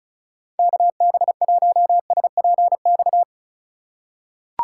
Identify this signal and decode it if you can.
Morse code, heard as a keyed tone, about 36 wpm, 700 Hz. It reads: KB1SPX
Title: USA Call Signs - 6 Characters - 4x Character Spacing - 35wpm